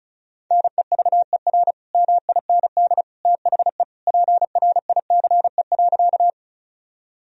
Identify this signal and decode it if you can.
Morse code, heard as a keyed tone, about 36 wpm, 700 Hz. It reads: NEVER MIND THE PRICE.
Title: Sentences from Top 1000 Words - Review Part 4 - 35wpm